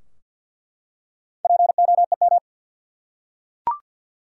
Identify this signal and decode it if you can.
Morse code, heard as a keyed tone, about 47 wpm, 700 Hz. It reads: POEM
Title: Four Letter Words - 50wpm